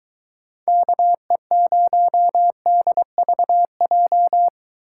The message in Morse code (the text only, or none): KE0DVJ